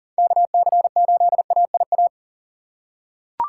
Morse code, heard as a keyed tone, about 40 wpm, 700 Hz. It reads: KC8AIA
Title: USA Call Signs - 6 Characters - 6x Character Spacing - 40wpm